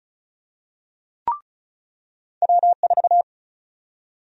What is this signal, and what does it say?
Morse code, heard as a keyed tone, about 36 wpm, 700 Hz. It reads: W4
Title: USA Call Sign Prefixes - 35wpm